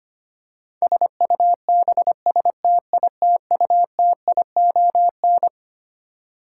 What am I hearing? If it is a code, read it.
Morse code, heard as a keyed tone, about 25 wpm, 700 Hz. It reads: SUBSTITUTION
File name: Twelve Letter Words - 25wpm